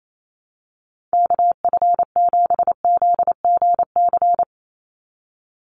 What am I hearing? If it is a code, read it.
Morse code, heard as a keyed tone, about 28 wpm, 700 Hz. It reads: KF7ZGC
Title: USA Call Signs 28wpm